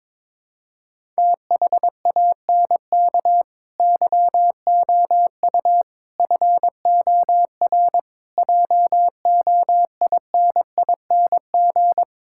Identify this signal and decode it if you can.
Morse code, heard as a keyed tone, about 22 wpm, 700 Hz. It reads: THANK YOU FOR JOINING
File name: practice / 22wpm / Build A Phrase - 5-Word Phrases - 22wpm